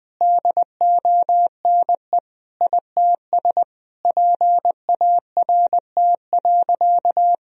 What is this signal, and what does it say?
Morse code, heard as a keyed tone, about 20 wpm, 700 Hz. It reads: DONE ITS PART.
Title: Sentences from Top 1000 Words - Review Part 3 - 20wpm